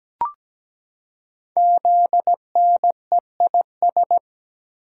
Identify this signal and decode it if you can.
Morse code, heard as a keyed tone, about 17 wpm, 700 Hz. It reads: ZNEIS
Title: Code Groups - 5 Letters - 17wpm